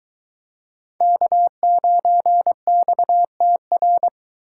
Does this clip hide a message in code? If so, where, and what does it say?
Yes: Morse code from 1.0 to 4.1 s, heard as a keyed tone, about 23 wpm, 700 Hz. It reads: K9XTR